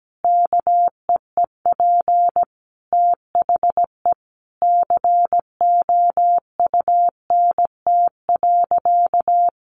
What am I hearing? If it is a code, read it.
Morse code, heard as a keyed tone, about 17 wpm, 700 Hz. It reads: KEEP THE COUNT.
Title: Sentences from Top 1500 Words 17wpm